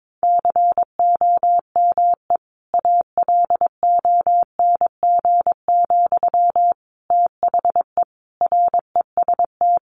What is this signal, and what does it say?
Morse code, heard as a keyed tone, about 22 wpm, 700 Hz. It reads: COME ALONG, THE REST
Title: Sentences from Top 700 Words 22wpm